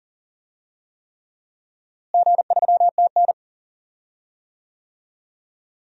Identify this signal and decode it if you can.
Morse code, heard as a keyed tone, about 40 wpm, 700 Hz. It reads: G3TN
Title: DXCC-100 Call Signs - 4 Characters - Repeated 3x - 40wpm